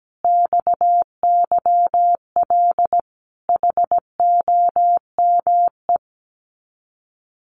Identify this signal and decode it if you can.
Morse code, heard as a keyed tone, about 17 wpm, 700 Hz. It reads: XYL HOME
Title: Common QSO Phrases Encore 17wpm